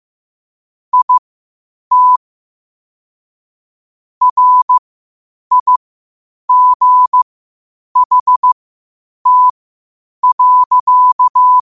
Morse code, heard as a keyed tone, about 15 wpm, spaced as for 8 wpm, 1000 Hz. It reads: IT RIGHT.
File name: qotdfiles / 8wpm